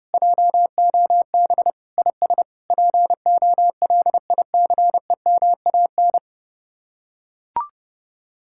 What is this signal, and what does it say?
Morse code, heard as a keyed tone, about 30 wpm, 700 Hz. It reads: JOB IS POLICEMAN